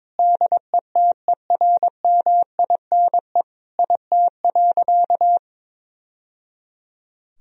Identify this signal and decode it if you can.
Morse code, heard as a keyed tone, about 22 wpm, 700 Hz. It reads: DETERMINE IT.
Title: Sentences from Top 1000 Words - Review Part 4 - 22wpm